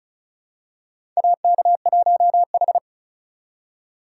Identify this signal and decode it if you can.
Morse code, heard as a keyed tone, about 34 wpm, 700 Hz. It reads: AK1H
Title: USA Call Signs - 4x Character Spacing - 35wpm